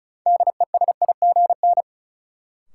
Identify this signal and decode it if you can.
Morse code, heard as a keyed tone, about 34 wpm, 700 Hz. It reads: DESIGN
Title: Sentences from Top 800 Words - New Word Drill - 35wpm